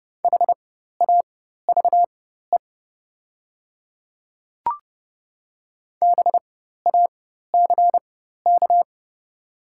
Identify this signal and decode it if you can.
Morse code, heard as a keyed tone, about 30 wpm, spaced as for 14 wpm, 690 Hz. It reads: HAVE BACK
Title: 4x Character Spacing - Top 200 Words - 30wpm